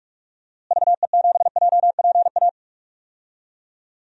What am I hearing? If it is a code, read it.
Morse code, heard as a keyed tone, about 46 wpm, 700 Hz. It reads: VE7JPA
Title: DXCC-100 Call Signs - 6 Characters - Repeated 2x - 45wpm